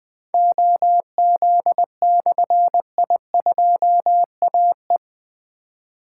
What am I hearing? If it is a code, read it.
Morse code, heard as a keyed tone, about 20 wpm, 700 Hz. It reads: OZ/I2AE
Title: Difficult Call Signs 20wpm